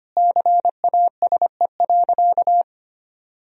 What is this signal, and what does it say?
Morse code, heard as a keyed tone, about 25 wpm, 700 Hz. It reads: CASE.
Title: Sentences from Top 1000 Words - Review Part 4 - 25wpm